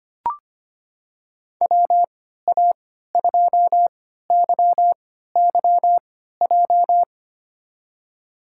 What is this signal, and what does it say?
Morse code, heard as a keyed tone, about 25 wpm, spaced as for 14 wpm, 700 Hz. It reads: WA2YYJ
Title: USA Call Signs - 6 Characters - 3x Character Spacing - 25wpm